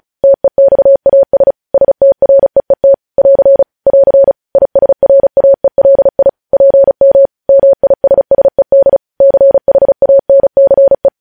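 Morse code, heard as a keyed tone, about 36 wpm, 550 Hz. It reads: TEXAS STREET <AR> <AR> ISRAELI PM MISSED CHANCE
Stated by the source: Morse code news headlines sent at 35 words per minute.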